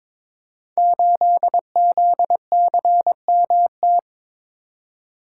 Morse code, heard as a keyed tone, about 22 wpm, 700 Hz. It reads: 8ZCMT
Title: Code Groups - 5 Letter:Numbers - 22wpm